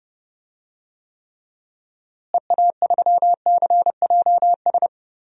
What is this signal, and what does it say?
Morse code, heard as a keyed tone, about 30 wpm, 700 Hz. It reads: EA3CJS
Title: DXCC-100 Call Signs - 6 Characters - Repeated 2x - 30wpm